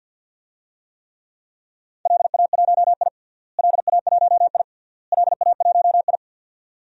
Morse code, heard as a keyed tone, about 50 wpm, 700 Hz. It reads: LA1I LA1I LA1I
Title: DXCC-100 Call Signs - 4 Characters - Repeated 3x - 50wpm